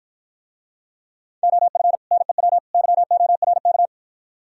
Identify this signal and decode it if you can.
Morse code, heard as a keyed tone, about 51 wpm, 690 Hz. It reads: OF NEW YORK